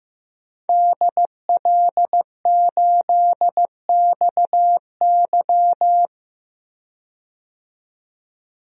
Morse code, heard as a keyed tone, about 15 wpm, 700 Hz. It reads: DL8XY
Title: DXCC-100 Call Signs - 5 Characters - 15wpm